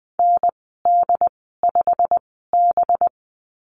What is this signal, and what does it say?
Morse code, heard as a keyed tone, about 20 wpm, spaced as for 14 wpm, 700 Hz. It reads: ND5B
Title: 193 Lesson 31g - Letter v - Call Signs 20wpm